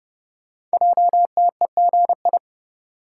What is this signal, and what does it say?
Morse code, heard as a keyed tone, about 30 wpm, 700 Hz. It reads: JTEGI
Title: Code Groups - 5 Letters - 30wpm